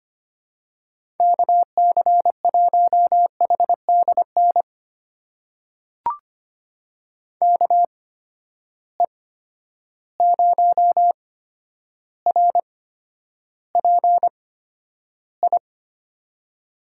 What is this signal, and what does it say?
Morse code, heard as a keyed tone, about 25 wpm, spaced as for 11 wpm, 700 Hz. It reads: KC1HDN KE0RPI
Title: USA Call Signs - 8x Character Spacing - 25wpm